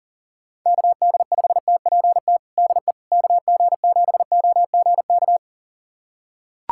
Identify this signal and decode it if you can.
Morse code, heard as a keyed tone, about 40 wpm, 700 Hz. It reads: KD5TPT DE KG7OGK
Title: USA Callsign DE Callsign 40wpm